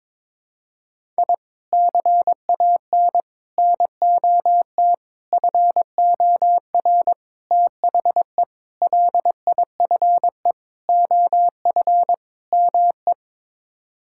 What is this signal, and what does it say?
Morse code, heard as a keyed tone, about 22 wpm, 700 Hz. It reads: I CAN NOT FOR THE LIFE OF ME
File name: practice / 22wpm / Sets of 8 Words - 22wpm